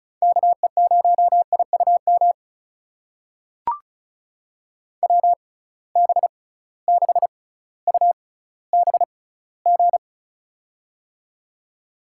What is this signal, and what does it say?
Morse code, heard as a keyed tone, about 36 wpm, spaced as for 18 wpm, 700 Hz. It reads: KE0IUM WB6UBG
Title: USA Call Signs - 6 Characters - 6x Character Spacing - 35wpm